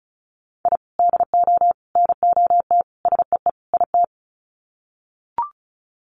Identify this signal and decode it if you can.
Morse code, heard as a keyed tone, about 34 wpm, 700 Hz. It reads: I DO NOT SEE IT
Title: Sets of 5 Words Encore2 35wpm